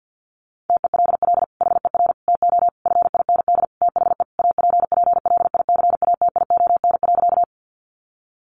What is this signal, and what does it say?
Morse code, heard as a keyed tone, about 50 wpm, 700 Hz. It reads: TELL HER TO FIND THE APPLICATION.
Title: Sentences from Top 1200 Words 50wpm